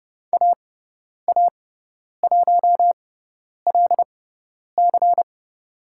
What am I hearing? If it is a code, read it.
Morse code, heard as a keyed tone, about 30 wpm, spaced as for 10 wpm, 700 Hz. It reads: AA1LC
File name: Lesson 16g - Letter c - Call Signs 30wpm